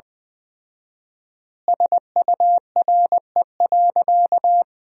Morse code, heard as a keyed tone, about 20 wpm, 700 Hz. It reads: SURE.
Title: 4x Word-Spacing - Sentences from Top 500 Words - 20wpm